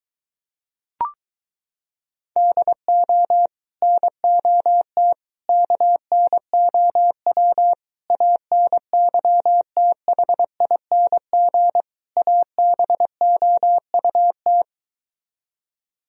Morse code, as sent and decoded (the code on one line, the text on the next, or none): -.. --- / -. --- - / -.- -. --- .-- / .- -. -.-- - .... .. -. --. / .- -... --- ..- -
DO NOT KNOW ANYTHING ABOUT